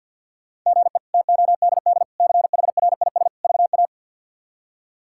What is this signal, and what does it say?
Morse code, heard as a keyed tone, about 52 wpm, 700 Hz. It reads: GE TODD CHRIS VA
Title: K1USN SST QSO Practice - 50wpm